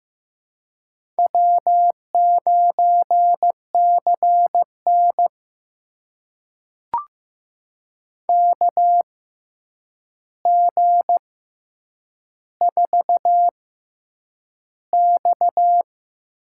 Morse code, heard as a keyed tone, about 15 wpm, spaced as for 8 wpm, 700 Hz. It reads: W9CN KG4X
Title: USA Call Signs - 4 Characters - 6x Character Spacing - 15wpm